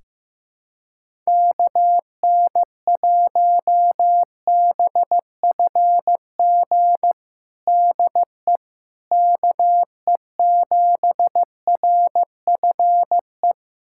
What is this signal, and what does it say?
Morse code, heard as a keyed tone, about 15 wpm, 700 Hz. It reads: KN1BFG DE KE7RFE
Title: USA Callsign DE Callsign - 6 Characters - 15wpm